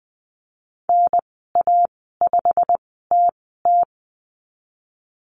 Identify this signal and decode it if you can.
Morse code, heard as a keyed tone, about 19 wpm, spaced as for 14 wpm, 700 Hz. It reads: NA5TT
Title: 088 Lesson 15g - Number 5 - Call Signs 20wpm